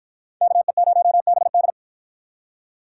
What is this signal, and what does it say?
Morse code, heard as a keyed tone, about 53 wpm, 700 Hz. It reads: KE0BD